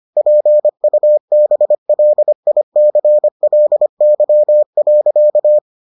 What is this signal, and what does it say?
Morse code, heard as a keyed tone, about 25 wpm, 600 Hz. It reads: PUBLICLY.